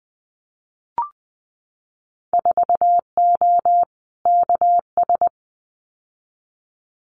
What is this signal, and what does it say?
Morse code, heard as a keyed tone, about 20 wpm, 700 Hz. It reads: 4O KS
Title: Winter Field Day Exchanges Encore 20wpm